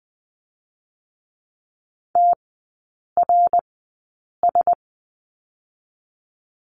Morse code, heard as a keyed tone, about 19 wpm, spaced as for 8 wpm, 700 Hz. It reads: TRS
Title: 121 Lesson 20e - Number 3 - Three Characters 20wpm